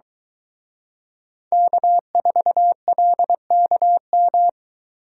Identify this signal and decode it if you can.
Morse code, heard as a keyed tone, about 23 wpm, 700 Hz. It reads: K4LKM